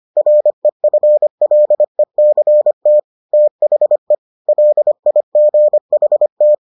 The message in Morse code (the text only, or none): REFLECT THE LIGHT